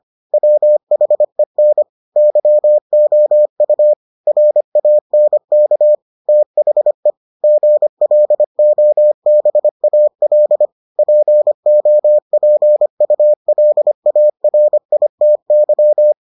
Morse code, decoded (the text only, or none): WHEN YOU RANK THE GLOBAL POPULARITY